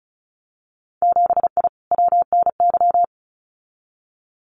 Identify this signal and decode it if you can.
Morse code, heard as a keyed tone, about 34 wpm, 700 Hz. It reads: 7I WNY
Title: Winter Field Day Exchanges Encore 35wpm